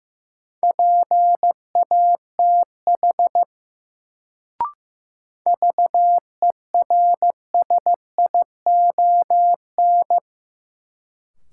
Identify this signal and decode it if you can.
Morse code, heard as a keyed tone, about 15 wpm, 700 Hz. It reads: PATH VERSION
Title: Sentences from Top 1400 Words - New Word Drill - 15wpm